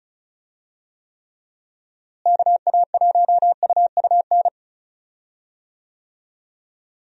Morse code, heard as a keyed tone, about 35 wpm, 700 Hz. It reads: KA1UUN
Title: USA Call Signs - 6 Characters - Repeated 2x - 35wpm